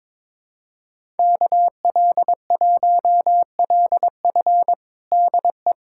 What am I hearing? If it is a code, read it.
Morse code, heard as a keyed tone, about 22 wpm, 700 Hz. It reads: KL1LF DE
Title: USA Callsign DE Callsign - 5 Characters - 22wpm